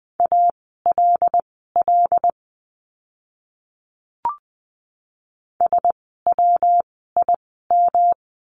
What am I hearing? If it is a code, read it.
Morse code, heard as a keyed tone, about 20 wpm, spaced as for 14 wpm, 700 Hz. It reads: ALL SWIM
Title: 115 Lesson 19f - Letter w - Words 20wpm